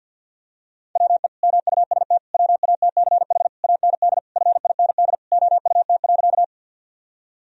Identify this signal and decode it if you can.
Morse code, heard as a keyed tone, about 46 wpm, 700 Hz. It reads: WE MUST WATCH AND FIND OUT.
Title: Sentences from Top 1000 Words - Review Part 1 - 50wpm